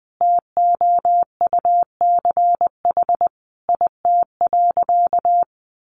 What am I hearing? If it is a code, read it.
Morse code, heard as a keyed tone, about 20 wpm, 700 Hz. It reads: TOUCH IT.
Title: Sentences from Top 1200 Words 20wpm